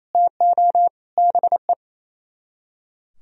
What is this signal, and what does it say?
Morse code, heard as a keyed tone, about 34 wpm, 700 Hz. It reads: TO BE
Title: Speed-Racing - Sets of 4 Words 28wpm